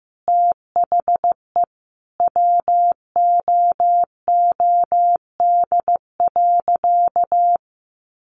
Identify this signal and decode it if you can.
Morse code, heard as a keyed tone, about 15 wpm, 700 Hz. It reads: THE WOOD.
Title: Sentences from Top 1100 Words 15wpm